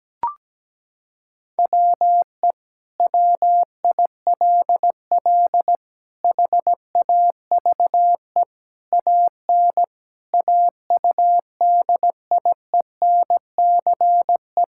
Morse code, heard as a keyed tone, about 17 wpm, 700 Hz. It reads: WE WILL HAVE AN AUDIENCE
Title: Sentences from Top 1000 Words - Review Part 5 - 17wpm